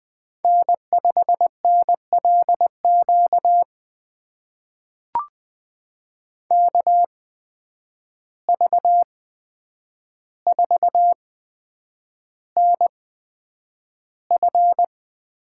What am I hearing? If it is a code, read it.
Morse code, heard as a keyed tone, about 20 wpm, spaced as for 9 wpm, 700 Hz. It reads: N5NLQ KV4NF
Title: USA Call Signs - 5 Characters - 8x Character Spacing - 20wpm